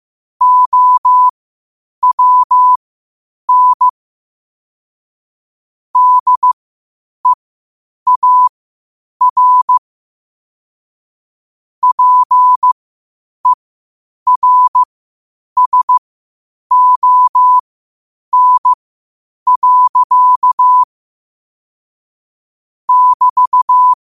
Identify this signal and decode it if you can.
Morse code, heard as a keyed tone, about 15 wpm, spaced as for 8 wpm, 1000 Hz. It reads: OWN DEAR PERSON. <BT>
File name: qotdfiles / 8wpm